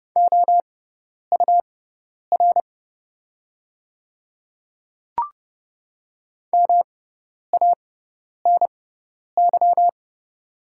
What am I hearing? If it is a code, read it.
Morse code, heard as a keyed tone, about 30 wpm, spaced as for 10 wpm, 700 Hz. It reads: OUR MANY
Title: 6x Character Spacing - Top 200 Words - 30wpm